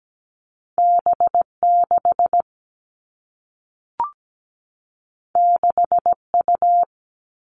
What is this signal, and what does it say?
Morse code, heard as a keyed tone, about 17 wpm, 700 Hz. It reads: B6 6U
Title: Two Challenging Letter-Number Combinations 17wpm